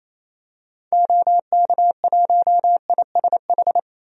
Morse code, heard as a keyed tone, about 28 wpm, 700 Hz. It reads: OK1ISH
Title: DXCC-100 Call Signs - 6 Characters - 28wpm